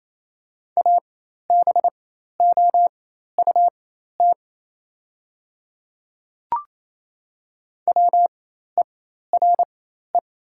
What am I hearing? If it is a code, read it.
Morse code, heard as a keyed tone, about 28 wpm, spaced as for 13 wpm, 700 Hz. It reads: ABOUT WERE
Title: 4x Character Spacing - Top 100 Words - 28wpm